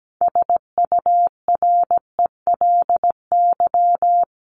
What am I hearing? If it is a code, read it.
Morse code, heard as a keyed tone, about 17 wpm, 700 Hz. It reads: SURELY
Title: LY Suffix - Top 100 Words with Base Words 17wpm